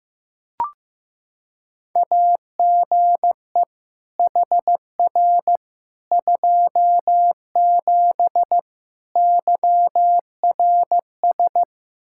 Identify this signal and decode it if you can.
Morse code, heard as a keyed tone, about 15 wpm, 700 Hz. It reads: AGE HR 27 YRS
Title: Practice QSO - Exchange 3 - More Info - 15wpm